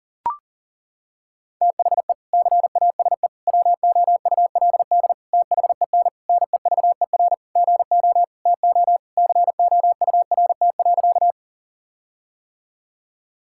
Morse code, heard as a keyed tone, about 40 wpm, 700 Hz. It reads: THE CASE WOULD THEN NEVER GO TO COURT.
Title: Sentences from Top 1600 Words 40wpm